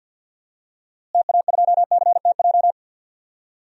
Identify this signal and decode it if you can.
Morse code, heard as a keyed tone, about 52 wpm, 700 Hz. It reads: TA2CTJ
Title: DXCC-100 Call Signs - 6 Characters - Repeated 2x - 50wpm